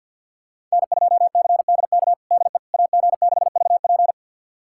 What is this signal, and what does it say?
Morse code, heard as a keyed tone, about 48 wpm, 700 Hz. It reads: N2CDK DE AG6VP
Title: USA Callsign DE Callsign - 5 Characters - 50wpm